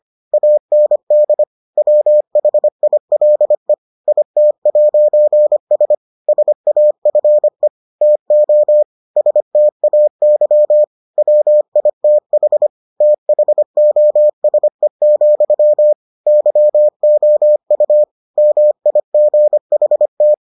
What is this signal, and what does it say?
Morse code, heard as a keyed tone, about 25 wpm, 600 Hz. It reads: AND WHILE IT'S SAFE TO STAY WITH THOSE, YOU MIGHT